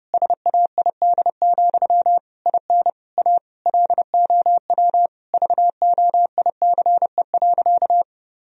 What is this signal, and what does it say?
Morse code, heard as a keyed tone, about 32 wpm, 700 Hz. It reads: SAID, IN A LOW VOICE.
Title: Sentences from Top 1000 Words - Review Part 1 - 30wpm